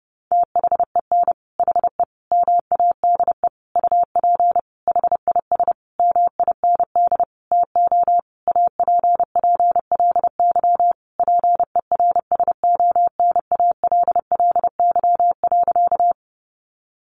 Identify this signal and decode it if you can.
Morse code, heard as a keyed tone, about 30 wpm, 700 Hz. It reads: THEN HE MADE UP HIS MIND TO APPLY PERSONALLY.
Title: Sentences from Top 1400 Words 30wpm